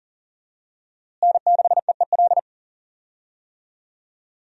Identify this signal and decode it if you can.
Morse code, heard as a keyed tone, about 40 wpm, 700 Hz. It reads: N6EEL
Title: USA Call Signs - 5 Characters - Repeated 2x - 40wpm